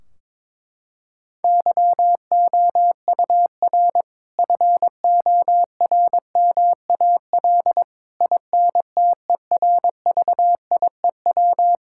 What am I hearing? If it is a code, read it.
Morse code, heard as a keyed tone, about 22 wpm, 700 Hz. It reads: YOUR FORMAL INTERVIEW
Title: Sentences from Top 1600 Words 22wpm